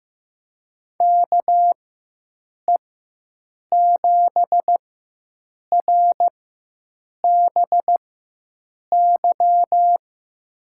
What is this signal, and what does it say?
Morse code, heard as a keyed tone, about 15 wpm, spaced as for 7 wpm, 700 Hz. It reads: KE7RBY
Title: USA Call Signs - 4x Character Spacing - 15wpm